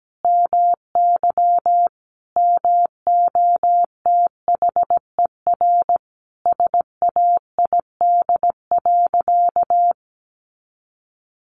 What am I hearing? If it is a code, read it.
Morse code, heard as a keyed tone, about 17 wpm, 700 Hz. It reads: MY MOTHER SAID.
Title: Sentences from Top 700 Words 17wpm